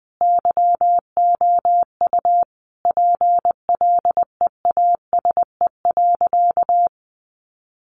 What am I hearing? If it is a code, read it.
Morse code, heard as a keyed tone, about 20 wpm, 700 Hz. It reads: YOU PLEASE.
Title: Sentences from Top 1500 Words 20wpm